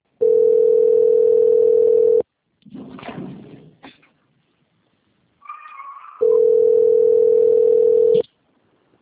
Ringback tone (North America), 440 + 480 Hz.
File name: out- - - - - .33124